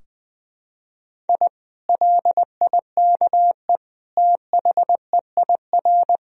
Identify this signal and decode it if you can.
Morse code, heard as a keyed tone, about 21 wpm, 700 Hz. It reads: I LIKE THEIR
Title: Sentences from Top 1000 Words - Review Part 4 - 20wpm